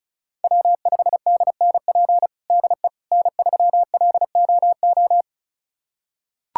Morse code, heard as a keyed tone, about 34 wpm, 700 Hz. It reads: W5DNP DE N3LOO
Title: USA Callsign DE Callsign - 5 Characters - 35wpm